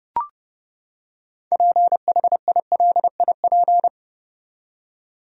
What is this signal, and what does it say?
Morse code, heard as a keyed tone, about 29 wpm, 700 Hz. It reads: PHILIP
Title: USA Common Names - 30wpm